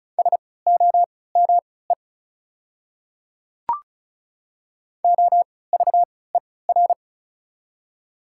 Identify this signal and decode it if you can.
Morse code, heard as a keyed tone, about 34 wpm, spaced as for 20 wpm, 700 Hz. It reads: SOME OVER
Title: 3x Character Spacing - Top 100 Words - 35wpm